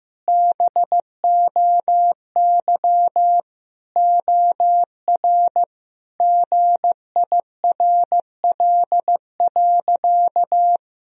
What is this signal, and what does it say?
Morse code, heard as a keyed tone, about 16 wpm, 700 Hz. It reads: BOY OR GIRL.
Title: Sentences from Top 1000 Words - Review Part 3 - 15wpm